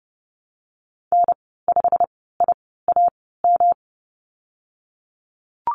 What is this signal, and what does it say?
Morse code, heard as a keyed tone, about 30 wpm, spaced as for 17 wpm, 700 Hz. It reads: N5IAM
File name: Lesson 19g - Letter w - Call Signs 30wpm